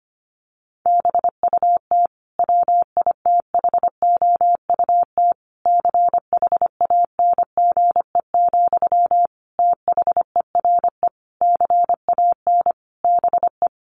Morse code, heard as a keyed tone, about 25 wpm, 700 Hz. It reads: BUT WITHOUT CHANGE, THERE CAN BE
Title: Sentences from Top 1400 Words 25wpm